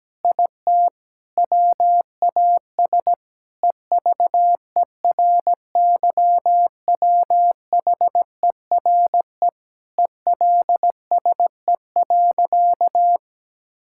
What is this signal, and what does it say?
Morse code, heard as a keyed tone, about 17 wpm, 700 Hz. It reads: IT WAS EVERYWHERE ELSE.
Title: Sentences from Top 1600 Words 17wpm